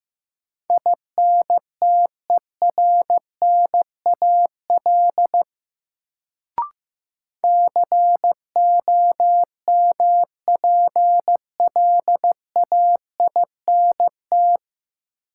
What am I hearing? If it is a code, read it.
Morse code, heard as a keyed tone, about 15 wpm, 700 Hz. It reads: INTERNAL COMPLAINT
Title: Sentences from Top 1400 Words - New Word Drill - 15wpm